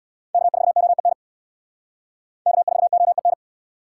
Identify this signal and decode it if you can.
Morse code, heard as a keyed tone, about 64 wpm, 700 Hz. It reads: K4OEN K4OEN